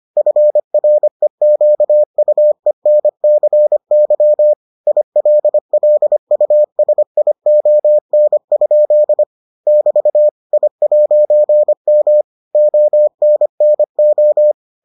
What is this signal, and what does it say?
Morse code, heard as a keyed tone, about 25 wpm, 600 Hz. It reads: FREQUENCY ILLUSION? <BT> I'M ONNO